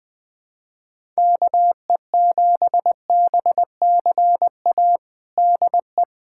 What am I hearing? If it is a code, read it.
Morse code, heard as a keyed tone, about 19 wpm, 700 Hz. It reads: KE7BCA DE
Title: USA Callsign DE Callsign 20wpm